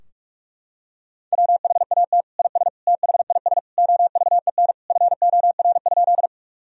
Morse code, heard as a keyed tone, about 41 wpm, 700 Hz. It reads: WHAT IS THIS OVEN FOR?
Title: Sentences from Top 1000 Words - Review Part 4 - 45wpm